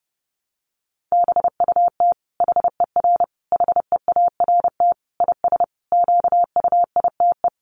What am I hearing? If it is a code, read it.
Morse code, heard as a keyed tone, about 30 wpm, 700 Hz. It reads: BUT HER HEART IS QUITE
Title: Sentences from Top 1200 Words 30wpm